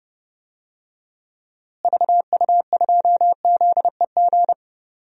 Morse code, heard as a keyed tone, about 31 wpm, 700 Hz. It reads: VU2ZEG
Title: DXCC-100 Call Signs - 6 Characters - Repeated 3x - 30wpm